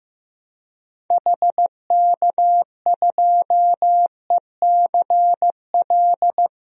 Morse code, heard as a keyed tone, about 15 wpm, 700 Hz. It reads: HK2ECL
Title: DXCC-100 Call Signs - 6 Characters - 15wpm